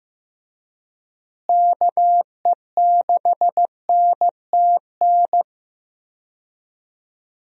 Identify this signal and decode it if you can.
Morse code, heard as a keyed tone, about 15 wpm, 700 Hz. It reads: KE6NTN
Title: USA Call Signs - 6 Characters - Repeated 3x - 15wpm